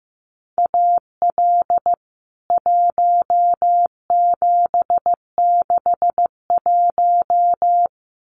Morse code, heard as a keyed tone, about 15 wpm, 700 Hz. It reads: AL 1761
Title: CWT Exchange 15wpm